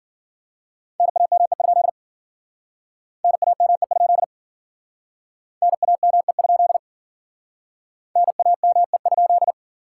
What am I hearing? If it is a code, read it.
Morse code, heard as a keyed tone, about 51 wpm, 700 Hz. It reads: NAME? NAME? NAME? NAME?